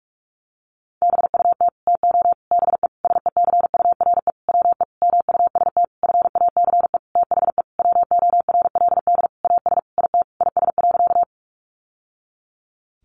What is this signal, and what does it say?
Morse code, heard as a keyed tone, about 46 wpm, 700 Hz. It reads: BUT TO BE SECURE WE MUST FACE THE WORLD AS IT IS.
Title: Sentences from Top 1200 Words 45wpm